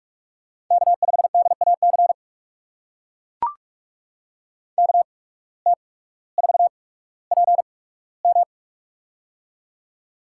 Morse code, heard as a keyed tone, about 44 wpm, spaced as for 19 wpm, 700 Hz. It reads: K5DAC KT4PM
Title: USA Call Signs - 5 Characters - 8x Character Spacing - 45wpm